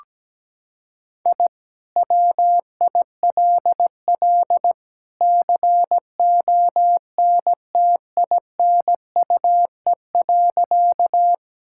Morse code, heard as a keyed tone, about 17 wpm, 700 Hz. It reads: I WILL CONTINUE.
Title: Sentences from Top 1000 Words - Review Part 6 - 17wpm